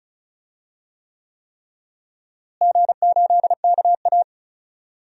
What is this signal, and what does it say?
Morse code, heard as a keyed tone, about 36 wpm, 700 Hz. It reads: G8KA
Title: DXCC-100 Call Signs - 4 Characters - Repeated 3x - 35wpm